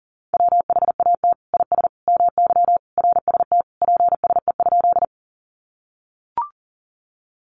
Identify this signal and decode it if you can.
Morse code, heard as a keyed tone, about 40 wpm, 700 Hz. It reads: WHAT IS MY RST PSE?
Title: Common QSO Phrases 40wpm